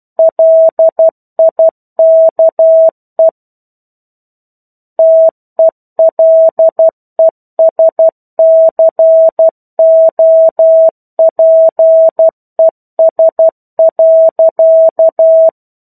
Morse code, heard as a keyed tone, about 12 wpm, 650 Hz. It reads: LIKE TELESCOPES.
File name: 01-Chapter-01 12 WPM